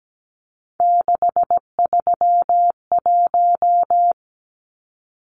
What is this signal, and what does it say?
Morse code, heard as a keyed tone, about 17 wpm, 700 Hz. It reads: 631
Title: Numbers - 3-Digit 17wpm